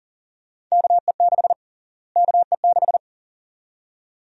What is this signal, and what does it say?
Morse code, heard as a keyed tone, about 40 wpm, 700 Hz. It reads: KE6 KE6
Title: USA Call Sign Prefixes - Repeated 2x - 40wpm